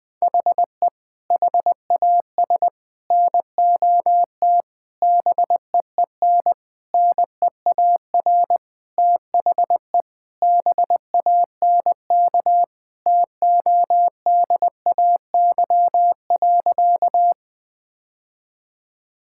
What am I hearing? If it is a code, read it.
Morse code, heard as a keyed tone, about 20 wpm, 700 Hz. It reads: HE HAS NOT BEEN NEAR THE BANK TODAY.
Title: Sentences from Top 1000 Words - Review Part 6 - 20wpm